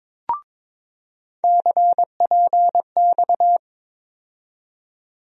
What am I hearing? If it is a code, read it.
Morse code, heard as a keyed tone, about 21 wpm, 700 Hz. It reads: CPX
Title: USA Call Sign Postfixes - 3 Letters 22wpm